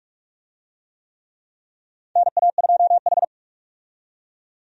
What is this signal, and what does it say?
Morse code, heard as a keyed tone, about 46 wpm, 700 Hz. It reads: NA2H
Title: USA Call Signs - 4 Characters - 45wpm